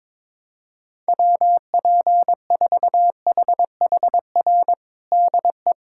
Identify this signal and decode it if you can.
Morse code, heard as a keyed tone, about 22 wpm, 700 Hz. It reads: WP4HHR DE
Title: USA Callsign DE Callsign 22wpm